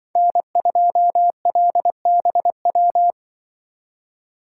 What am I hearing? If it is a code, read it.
Morse code, heard as a keyed tone, about 25 wpm, 700 Hz. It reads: N2LBW
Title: Speed Racing - USA 1x3 Call Signs 20wpm